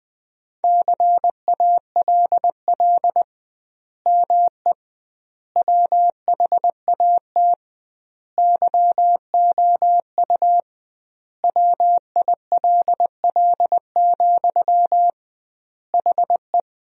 Morse code, heard as a keyed tone, about 20 wpm, 700 Hz. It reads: CALL ME WHAT YOU WILL, HE
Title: 2x Word-Spacing - Sentences from Top 200 Words - 20wpm